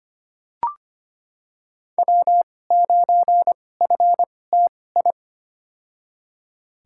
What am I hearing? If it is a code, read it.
Morse code, heard as a keyed tone, about 25 wpm, spaced as for 18 wpm, 700 Hz. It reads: W9FTI
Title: USA Call Signs - 2x Character Spacing - 25wpm